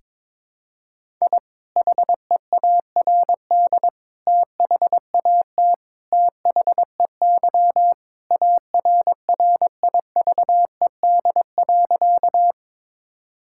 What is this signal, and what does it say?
Morse code, heard as a keyed tone, about 21 wpm, 700 Hz. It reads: I HEARD THAT THEY ARRIVED.
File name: practice / 22wpm / Sentences with 2-Word Phrases - 22wpm